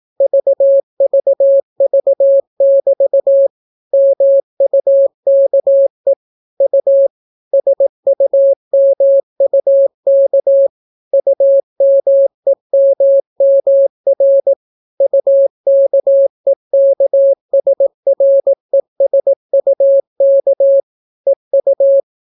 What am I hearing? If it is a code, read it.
Morse code, heard as a keyed tone, about 18 wpm, 550 Hz. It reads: VVV<BT> MUKE U SUMUK UMEMMR UKEKSRESUK EU